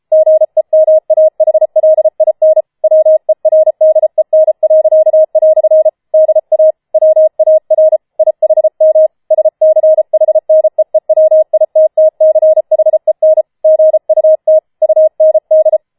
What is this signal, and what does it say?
Morse code, heard as a keyed tone, about 33 wpm, 630 Hz. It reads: GEMAHLIN WERDEN." DA WAR IHM SCHNEEWITTCHEN GUT UND